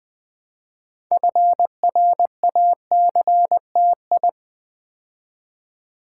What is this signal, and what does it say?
Morse code, heard as a keyed tone, about 20 wpm, 700 Hz. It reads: FRACTI
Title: Build A Word - 2 Letters At A Time - 8-Letter Words - 20wpm